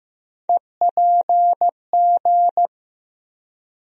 Morse code, heard as a keyed tone, about 15 wpm, 700 Hz. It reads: EPG
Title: USA Call Sign Postfixes - 3 Letters 15wpm